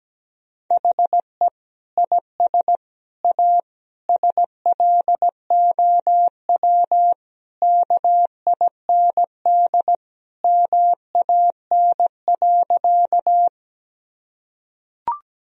Morse code, heard as a keyed tone, about 17 wpm, 700 Hz. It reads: HE IS A SLOW KIND MAN.
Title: Sentences from Top 1000 Words - Review Part 2 - 17wpm